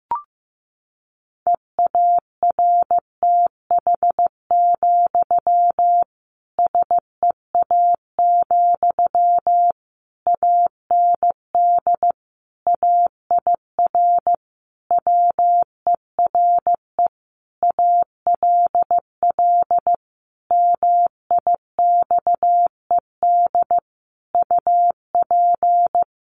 Morse code, heard as a keyed tone, about 15 wpm, 700 Hz. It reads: EARTH, SEA, AND AIR WERE ALL MIXED UP
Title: Sentences from Top 1500 Words 15wpm